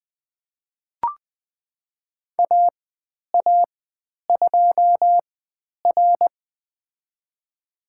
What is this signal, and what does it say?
Morse code, heard as a keyed tone, about 20 wpm, spaced as for 10 wpm, 700 Hz. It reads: AA2R
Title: 102 Lesson 17g - Letter u - Call Signs 20wpm